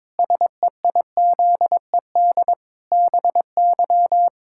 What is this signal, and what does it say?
Morse code, heard as a keyed tone, about 22 wpm, 700 Hz. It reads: SEIZED BY
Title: Pangram Challenge v2 - 22wpm